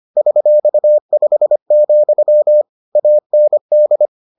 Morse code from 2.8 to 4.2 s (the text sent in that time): AND